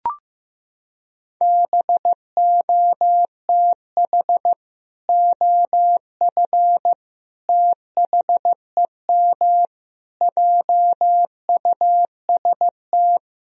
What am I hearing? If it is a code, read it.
Morse code, heard as a keyed tone, about 16 wpm, 700 Hz. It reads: BOTH OF THEM JUST
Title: Sentences from Top 1000 Words - Review Part 2 - 15wpm